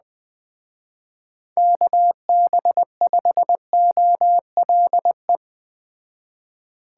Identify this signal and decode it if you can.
Morse code, heard as a keyed tone, about 20 wpm, 700 Hz. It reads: KB5OLE
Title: Speed Racing - USA 2x3 Call Signs 17wpm